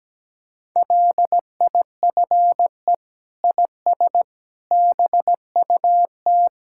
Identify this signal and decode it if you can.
Morse code, heard as a keyed tone, about 17 wpm, 700 Hz. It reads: LIFE IS BUT
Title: Sentences from Top 1600 Words 17wpm